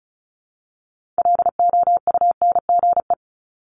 Morse code, heard as a keyed tone, about 36 wpm, 700 Hz. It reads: LOUNGE
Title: Multi-letter Phonemes Repeated 3x 35wpm